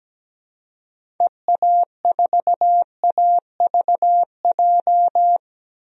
Morse code, heard as a keyed tone, about 17 wpm, 700 Hz. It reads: EA4AVJ
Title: DXCC-100 Call Signs - 6 Characters - 17wpm